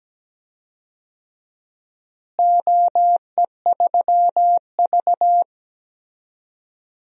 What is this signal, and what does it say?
Morse code, heard as a keyed tone, about 17 wpm, 700 Hz. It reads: OE3V
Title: DXCC-100 Call Signs - 4 Characters - Repeated 3x - 17wpm